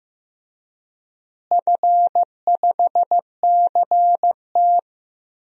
Morse code, heard as a keyed tone, about 15 wpm, 700 Hz. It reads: F5CT
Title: DXCC-100 Call Signs - 4 Characters - Repeated 2x - 15wpm